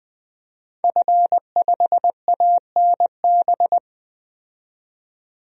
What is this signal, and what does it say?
Morse code, heard as a keyed tone, about 20 wpm, 700 Hz. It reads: F5ANB
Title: DXCC-100 Call Signs - 5 Characters - 20wpm